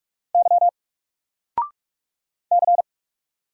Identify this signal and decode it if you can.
Morse code, heard as a keyed tone, about 44 wpm, 700 Hz. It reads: Y C
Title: Single Letter-Number 45wpm